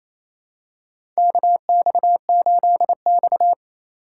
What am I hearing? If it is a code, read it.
Morse code, heard as a keyed tone, about 28 wpm, 700 Hz. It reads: KX8X
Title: K1USN SST QSO Practice - 28wpm